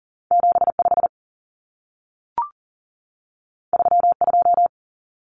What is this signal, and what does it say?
Morse code, heard as a keyed tone, about 40 wpm, 700 Hz. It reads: 75 32
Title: Common Cut Numbers - 2-Digit 40wpm